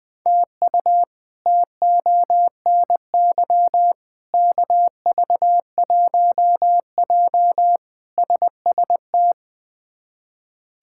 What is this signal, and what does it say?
Morse code, heard as a keyed tone, about 20 wpm, 700 Hz. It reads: TU TONY KV1J SST
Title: K1USN SST QSO Practice - 20wpm